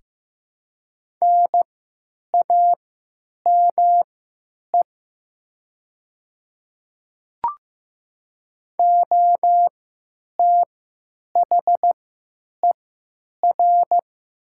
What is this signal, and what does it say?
Morse code, heard as a keyed tone, about 15 wpm, spaced as for 9 wpm, 700 Hz. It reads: NAME OTHER
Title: 3x Character Spacing - Top 300 Words - 15wpm